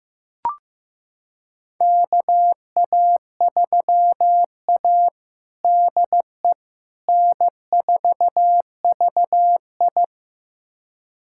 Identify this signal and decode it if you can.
Morse code, heard as a keyed tone, about 15 wpm, 700 Hz. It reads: KA3A DE N4VI
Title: USA Callsign DE Callsign - 4 Characters - 15wpm